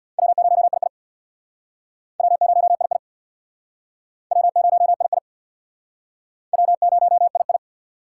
Morse code, heard as a keyed tone, about 59 wpm, 700 Hz. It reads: W0II W0II W0II W0II
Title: Speed Racing - USA 1x2 Call Signs 50wpm